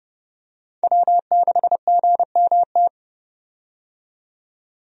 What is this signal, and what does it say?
Morse code, heard as a keyed tone, about 32 wpm, 700 Hz. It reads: W6GMT
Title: USA Call Signs - 5 Characters - 30wpm